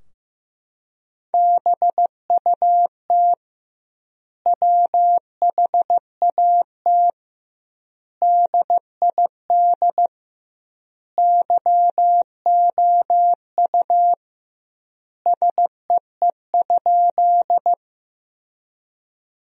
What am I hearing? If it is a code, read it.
Morse code, heard as a keyed tone, about 15 wpm, 700 Hz. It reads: BUT WHAT DID YOU SEE?
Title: 2x Word-Spacing - Sentences from Top 100 Words - 15wpm